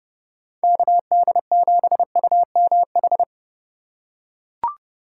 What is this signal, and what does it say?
Morse code, heard as a keyed tone, about 30 wpm, 700 Hz. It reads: KD7UMH
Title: USA Call Signs - 6 Characters - 4x Character Spacing - 30wpm